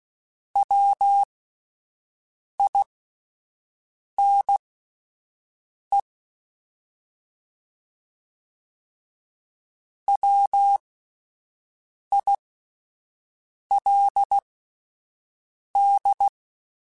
Morse code, heard as a keyed tone, about 16 wpm, spaced as for 5 wpm, 800 Hz. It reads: WINE WILD
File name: Alphabetical / Alphabetical mp3s / Ws @5